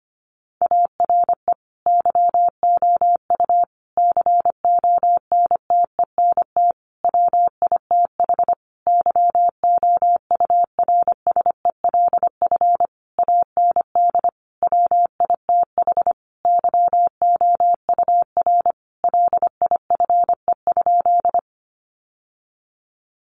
Morse code, heard as a keyed tone, about 25 wpm, 700 Hz. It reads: ARE YOU CONTENT WITH YOURSELF AND WITH YOUR LIFE?
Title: Sentences from Top 900 Words 25wpm